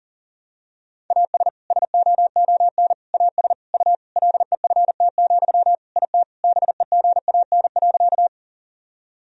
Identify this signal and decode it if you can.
Morse code, heard as a keyed tone, about 40 wpm, 700 Hz. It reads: AS SOON AS U LEFT, IT BEGAN.
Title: Sentences from Top 400 Words and Abbreviations 40wpm